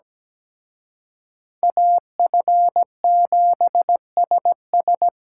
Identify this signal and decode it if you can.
Morse code, heard as a keyed tone, about 20 wpm, 700 Hz. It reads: AF7SS
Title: Speed Racing - USA 2x2 Call Signs 17wpm